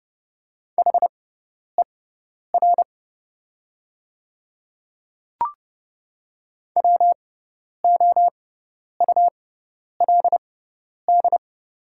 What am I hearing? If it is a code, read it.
Morse code, heard as a keyed tone, about 30 wpm, spaced as for 10 wpm, 700 Hz. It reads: HER WOULD
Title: 6x Character Spacing - Top 100 Words - 30wpm